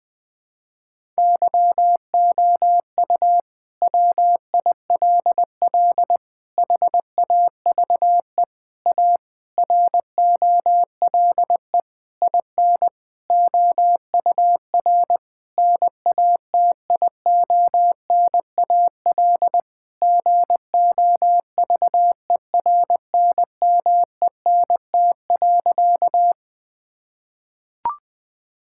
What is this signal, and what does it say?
Morse code, heard as a keyed tone, about 20 wpm, 700 Hz. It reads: YOU WILL HAVE A ROLE IN OUR NATIONAL GOVERNMENT.
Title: Sentences from Top 1000 Words - Review Part 6 - 20wpm